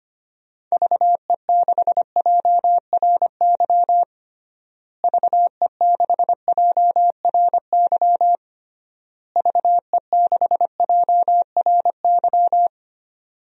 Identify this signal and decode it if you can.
Morse code, heard as a keyed tone, about 25 wpm, 700 Hz. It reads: VE6JRY VE6JRY VE6JRY
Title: DXCC-100 Call Signs - 6 Characters - Repeated 3x - 25wpm